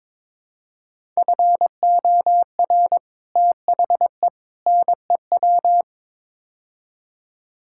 Morse code, heard as a keyed tone, about 22 wpm, 700 Hz. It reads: FOR THE NEW
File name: practice / 22wpm / Build A Phrase - 5-Word Phrases - 22wpm